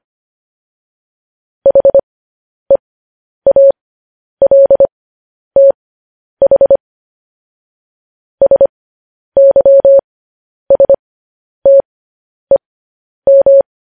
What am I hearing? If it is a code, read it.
Morse code, heard as a keyed tone, about 25 wpm, spaced as for 10 wpm, 550 Hz. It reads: HEALTH SYSTEM
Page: news headlines in morse code at 10 wpm